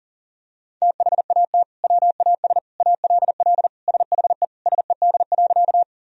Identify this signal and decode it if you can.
Morse code, heard as a keyed tone, about 40 wpm, 700 Hz. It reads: THAT WAS ALL SHE SED.
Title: Sentences from Top 100 Words and Abbreviations 40wpm